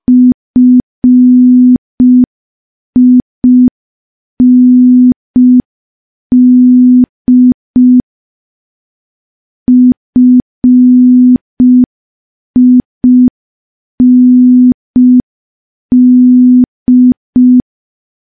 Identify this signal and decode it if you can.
Morse code, heard as a keyed tone, about 5 wpm, 250 Hz. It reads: FIND FIND
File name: morse / morse-training-05WPM